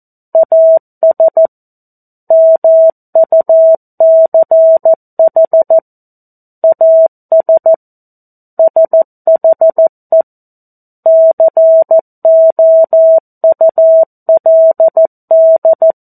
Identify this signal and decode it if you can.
Morse code, heard as a keyed tone, about 14 wpm, 650 Hz. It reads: AS MUCH AS SHE COULD
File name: 02-Chapter-02 14 WPM